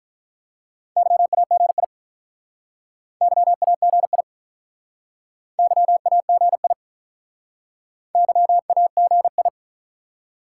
Morse code, heard as a keyed tone, about 47 wpm, 700 Hz. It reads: YAGI YAGI YAGI YAGI